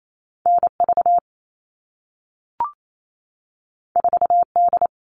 Morse code, heard as a keyed tone, about 28 wpm, 700 Hz. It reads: NV 4D
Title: All Cut Numbers - 2-Digit 28wpm